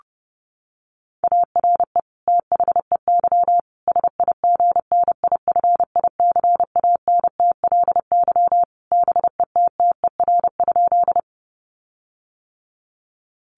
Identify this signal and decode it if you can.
Morse code, heard as a keyed tone, about 30 wpm, 700 Hz. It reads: ARE THEY SIGNIFICANTLY BETTER?
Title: Sentences from Top 1500 Words 30wpm